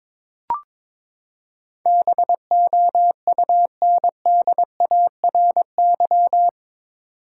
Morse code, heard as a keyed tone, about 21 wpm, 700 Hz. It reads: BOUNDARY
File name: practice / 22wpm / Eight Letter Words - 22wpm